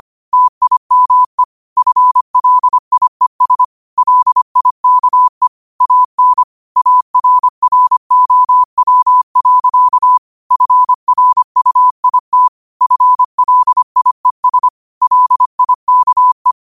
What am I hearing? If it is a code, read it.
Morse code, heard as a keyed tone, about 25 wpm, 1000 Hz. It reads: TIME FLIES LIKE AN ARROW. FRUIT FLIES LIKE